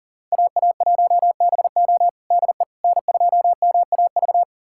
Morse code, heard as a keyed tone, about 37 wpm, 700 Hz. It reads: AA1BO DE N2MAV
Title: USA Callsign DE Callsign - 5 Characters - 40wpm